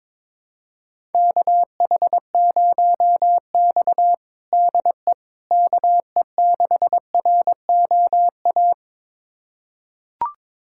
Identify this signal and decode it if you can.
Morse code, heard as a keyed tone, about 22 wpm, 700 Hz. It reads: KH0X DE KE6ROA
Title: USA Callsign DE Callsign 22wpm